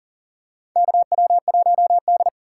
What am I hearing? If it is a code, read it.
Morse code, heard as a keyed tone, about 40 wpm, 700 Hz. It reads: KW1D
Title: USA Call Signs - 4 Characters - 6x Character Spacing - 40wpm